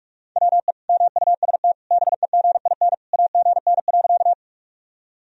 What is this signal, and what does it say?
Morse code, heard as a keyed tone, about 46 wpm, 700 Hz. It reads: WE MUST BEGIN AGN.
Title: Sentences from Top 300 Words and Abbreviations 45wpm